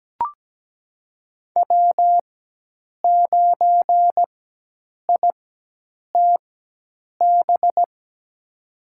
Morse code, heard as a keyed tone, about 17 wpm, spaced as for 8 wpm, 700 Hz. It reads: W9ITB
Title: USA Call Signs - 4x Character Spacing - 17wpm